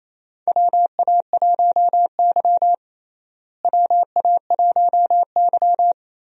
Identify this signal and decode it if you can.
Morse code, heard as a keyed tone, about 28 wpm, 700 Hz. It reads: WA1Y WA1Y
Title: USA Call Signs - 4 Characters - Repeated 2x - 28wpm